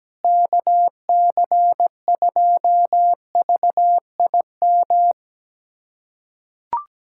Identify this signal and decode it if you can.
Morse code, heard as a keyed tone, about 17 wpm, 700 Hz. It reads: KC2VIM
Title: USA Callsign DE Callsign - 6 Characters - 17wpm